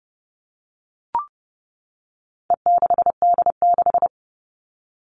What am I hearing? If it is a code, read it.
Morse code, heard as a keyed tone, about 30 wpm, 700 Hz. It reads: E6D6
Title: All Cut Numbers - 4-Digit 30wpm